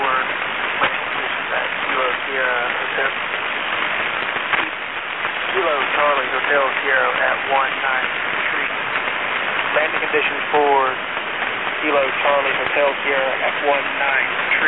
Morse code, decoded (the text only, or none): none